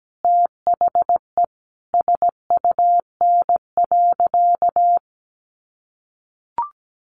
Morse code, heard as a keyed tone, about 17 wpm, 700 Hz. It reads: THE SUN.
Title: Sentences from Top 500 Words 17wpm